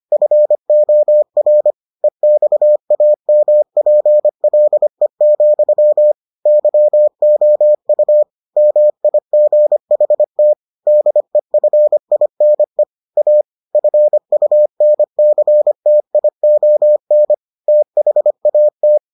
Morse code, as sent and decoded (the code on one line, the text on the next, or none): ..-. --- .-. / . -..- .- -- .--. .-.. --..-- / -.-- --- ..- / -- .. --. .... - / -.. ..-. .. -. / .- / ..-. ..- -. -.-. - .. --- -. / - .... .- -
FOR EXAMPLE, YOU MIGHT DEFINE A FUNCTION THAT